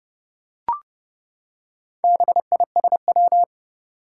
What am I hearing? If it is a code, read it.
Morse code, heard as a keyed tone, about 30 wpm, 700 Hz. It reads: BISW
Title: Four Challenging Letter-Number Combinations 30wpm